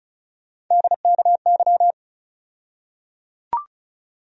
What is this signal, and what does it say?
Morse code, heard as a keyed tone, about 34 wpm, 700 Hz. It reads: DKY
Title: USA Call Sign Postfixes - 3 Letters 35wpm